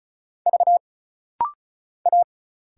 Morse code, heard as a keyed tone, about 36 wpm, 700 Hz. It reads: V A
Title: Single Letter-Number Alt - Rapid-Fire 35wpm